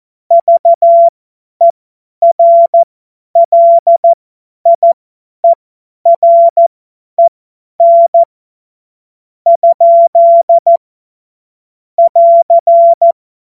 Morse code, heard as a keyed tone, about 14 wpm, spaced as for 10 wpm, 680 Hz. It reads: VERLIEREN ? <AR>
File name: Morsekurs / Witze / Witz 14.November 2023 50 F70